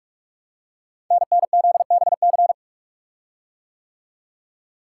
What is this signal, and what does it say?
Morse code, heard as a keyed tone, about 44 wpm, 690 Hz. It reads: NNZBC
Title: Code Groups - 5 Letters - 45wpm